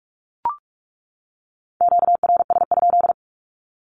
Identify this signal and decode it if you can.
Morse code, heard as a keyed tone, about 44 wpm, 700 Hz. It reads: QRS?
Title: Q-Codes 45wpm